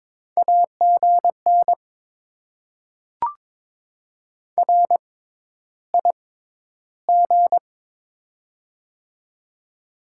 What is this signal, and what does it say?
Morse code, heard as a keyed tone, about 22 wpm, spaced as for 11 wpm, 700 Hz. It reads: AGN RIG
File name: Top 20 QSO Elements - 6x Character Spacing - 22wpm